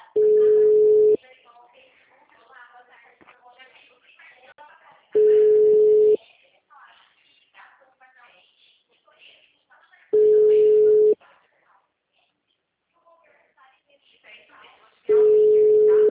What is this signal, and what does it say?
Ringback tone (Europe, 425 Hz), with speech.